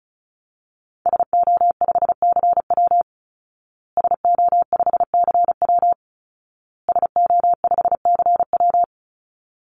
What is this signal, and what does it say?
Morse code, heard as a keyed tone, about 34 wpm, 700 Hz. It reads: SO5CW SO5CW SO5CW
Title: Call Signs Repeated 2x - 35wpm